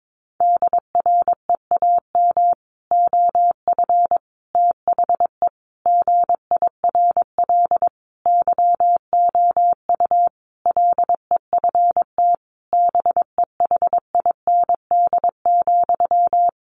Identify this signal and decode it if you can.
Morse code, heard as a keyed tone, about 22 wpm, 700 Hz. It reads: DREAM OF THE GIRL YOU LEFT BEHIND,
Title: Sentences from Top 1500 Words 22wpm